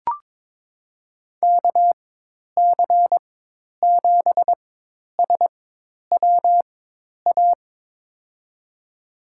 Morse code, heard as a keyed tone, about 22 wpm, spaced as for 10 wpm, 700 Hz. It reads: KC7SWA
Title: USA Call Signs - 6 Characters - 4x Character Spacing - 22wpm